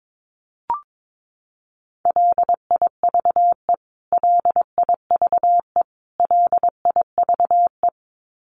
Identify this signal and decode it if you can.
Morse code, heard as a keyed tone, about 22 wpm, 700 Hz. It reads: LIVE LIVE LIVE
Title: Top 500 Words Repeated 22wpm